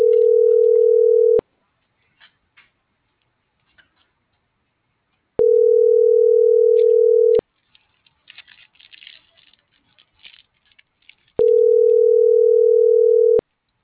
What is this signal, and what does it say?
Ringback tone (North America), 440 + 480 Hz.